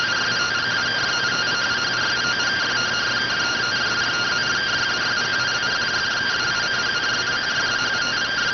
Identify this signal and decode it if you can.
Two-tone teleprinter signal, about 170 Hz shift.